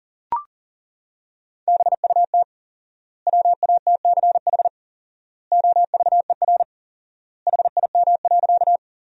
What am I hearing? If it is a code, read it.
Morse code, heard as a keyed tone, about 37 wpm, 700 Hz. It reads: BUT WATCH OVER HIM.
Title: 4x Word-Spacing - Sentences from Top 300 Words - 40wpm